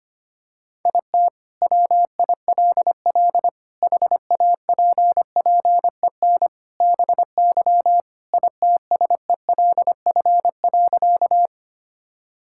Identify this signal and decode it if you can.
Morse code, heard as a keyed tone, about 25 wpm, 700 Hz. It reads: IT WILL HAPPEN BY ITSELF.
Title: Sentences from Top 1000 Words - Review Part 5 - 25wpm